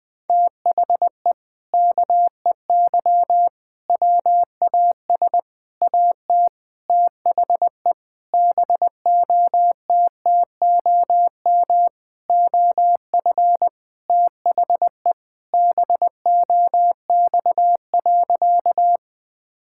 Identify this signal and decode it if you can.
Morse code, heard as a keyed tone, about 20 wpm, 700 Hz. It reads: THE KEY WAS AT THE BOTTOM OF THE BOX.
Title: Sentences with 4-Word Phrases - 20wpm